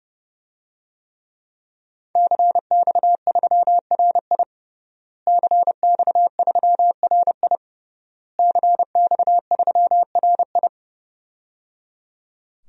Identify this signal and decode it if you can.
Morse code, heard as a keyed tone, about 30 wpm, 700 Hz. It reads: CX3RI CX3RI CX3RI
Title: DXCC-100 Call Signs - 5 Characters - Repeated 3x - 30wpm